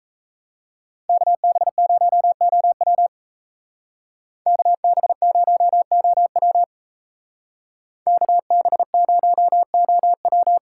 Morse code, heard as a keyed tone, about 33 wpm, 700 Hz. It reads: KB0OW KB0OW KB0OW
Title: Speed Racing - USA 2x2 Call Signs 28wpm